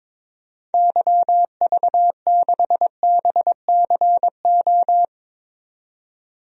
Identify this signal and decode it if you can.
Morse code, heard as a keyed tone, about 22 wpm, 700 Hz. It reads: YV6BCO
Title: DXCC-100 Call Signs - 6 Characters - 22wpm